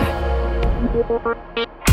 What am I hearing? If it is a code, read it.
There is no code.